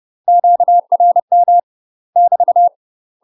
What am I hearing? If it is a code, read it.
Morse code, heard as a keyed tone, about 30 wpm, 700 Hz. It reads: QRM <BT>
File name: UK MASTER QSOs TRAINING / 30WPM / UK MASTER QSOs 0039